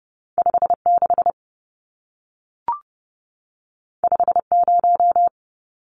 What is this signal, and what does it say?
Morse code, heard as a keyed tone, about 30 wpm, 700 Hz. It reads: 56 50
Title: Numbers - 2-Digit 30wpm